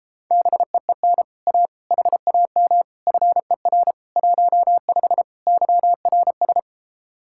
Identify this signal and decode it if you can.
Morse code, heard as a keyed tone, about 33 wpm, 700 Hz. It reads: BEEN A HAM FER 15 YRS